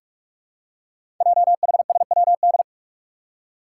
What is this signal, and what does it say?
Morse code, heard as a keyed tone, about 44 wpm, 700 Hz. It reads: JHSWD
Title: Code Groups - 5 Letters - 45wpm